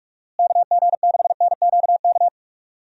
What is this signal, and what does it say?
Morse code, heard as a keyed tone, about 42 wpm, 700 Hz. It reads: KG6NQK
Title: USA Call Signs - 6 Characters - Repeated 2x - 45wpm